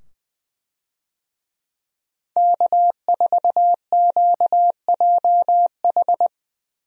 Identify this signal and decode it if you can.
Morse code, heard as a keyed tone, about 20 wpm, 700 Hz. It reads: K4QJH
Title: USA Call Signs - 5 Characters - 20wpm